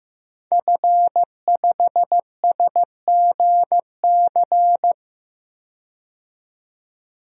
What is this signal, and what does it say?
Morse code, heard as a keyed tone, about 15 wpm, 700 Hz. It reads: F5SGC
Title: DXCC-100 Call Signs - 5 Characters - 15wpm